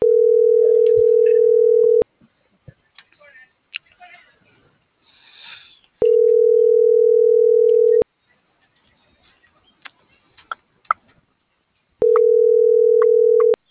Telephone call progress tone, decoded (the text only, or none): ringback tone